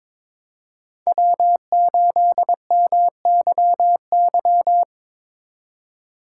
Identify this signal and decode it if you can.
Morse code, heard as a keyed tone, about 25 wpm, 700 Hz. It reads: W8MYY
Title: Speed Racing - USA 1x3 Call Signs 22wpm